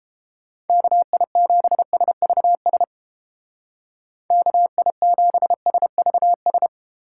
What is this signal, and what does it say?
Morse code, heard as a keyed tone, about 30 wpm, 700 Hz. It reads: KI7SVS KI7SVS